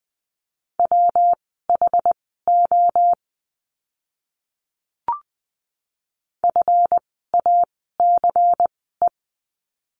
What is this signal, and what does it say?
Morse code, heard as a keyed tone, about 20 wpm, spaced as for 15 wpm, 700 Hz. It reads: WHO FACE
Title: 139 Lesson 23f - Letter f - Words 20wpm